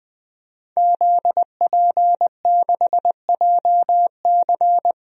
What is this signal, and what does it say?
Morse code, heard as a keyed tone, about 20 wpm, 700 Hz. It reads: ZP6JC
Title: DXCC-100 Call Signs - 5 Characters - 20wpm